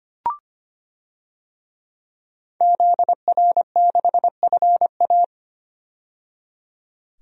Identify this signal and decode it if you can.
Morse code, heard as a keyed tone, about 25 wpm, 700 Hz. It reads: ZR6FA
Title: DXCC-100 Call Signs - 5 Characters - 25wpm